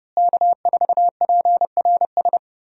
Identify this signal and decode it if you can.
Morse code, heard as a keyed tone, about 31 wpm, 700 Hz. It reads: K4PRS
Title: USA Call Signs - 5 Characters - Repeated 3x - 30wpm